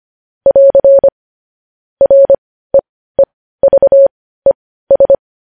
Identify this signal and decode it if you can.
Morse code, heard as a keyed tone, about 25 wpm, spaced as for 15 wpm, 550 Hz. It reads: <AR> REEVES
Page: news headlines in morse code at 15 wpm